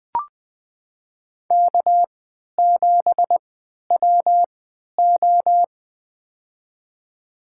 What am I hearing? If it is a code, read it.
Morse code, heard as a keyed tone, about 20 wpm, spaced as for 11 wpm, 700 Hz. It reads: K7WO
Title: USA Call Signs - 4 Characters - 3x Character Spacing - 20wpm